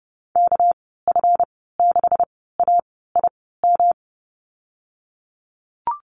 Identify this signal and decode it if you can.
Morse code, heard as a keyed tone, about 30 wpm, spaced as for 17 wpm, 700 Hz. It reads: KF6AIM
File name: Lesson 37g - Letter x - Call Signs 30wpm